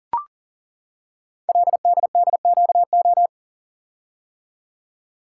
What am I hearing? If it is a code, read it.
Morse code, heard as a keyed tone, about 38 wpm, 700 Hz. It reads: LDDQO
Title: Code Groups - 5 Letters - 40wpm